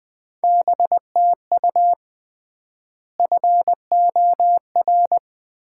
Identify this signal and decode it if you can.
Morse code, heard as a keyed tone, about 21 wpm, 700 Hz. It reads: BTU FOR
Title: Common QSO Phrases - 3x Word-Spacing - 20wpm